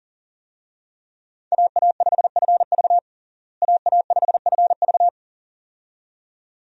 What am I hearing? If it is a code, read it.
Morse code, heard as a keyed tone, about 40 wpm, 700 Hz. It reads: AA5FV AA5FV
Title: USA Call Signs - 5 Characters - Repeated 2x - 40wpm